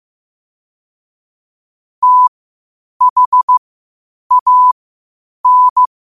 Morse code, heard as a keyed tone, about 15 wpm, spaced as for 8 wpm, 1000 Hz. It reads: THAN
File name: qotdfiles / 8wpm